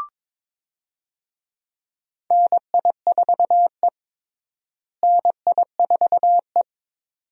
Morse code, heard as a keyed tone, about 22 wpm, 700 Hz. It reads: NI4E NI4E
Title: USA Call Signs - 4 Characters - Repeated 2x - 22wpm